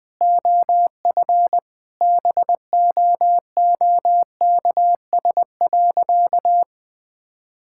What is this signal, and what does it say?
Morse code, heard as a keyed tone, about 20 wpm, 700 Hz. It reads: OF BOOKS.
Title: Sentences with 3-Word Phrases - 20wpm